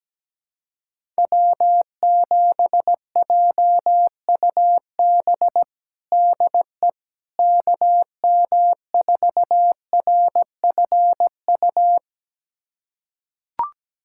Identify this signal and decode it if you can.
Morse code, heard as a keyed tone, about 17 wpm, 700 Hz. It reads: W7JUB DE KM4RFU
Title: USA Callsign DE Callsign 17wpm